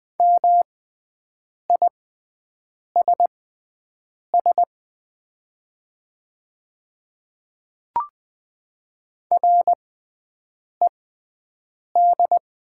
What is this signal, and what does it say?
Morse code, heard as a keyed tone, about 20 wpm, spaced as for 7 wpm, 700 Hz. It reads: MISS RED
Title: 6x Character Spacing - Top 500 Words - 20wpm